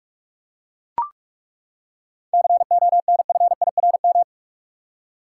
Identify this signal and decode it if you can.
Morse code, heard as a keyed tone, about 41 wpm, 700 Hz. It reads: CONFIRM
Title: Seven Letter Words - 45wpm